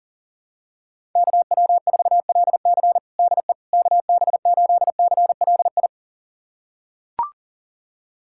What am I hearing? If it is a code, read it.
Morse code, heard as a keyed tone, about 37 wpm, 700 Hz. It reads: KW4LC DE KB8CLI
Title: USA Callsign DE Callsign 40wpm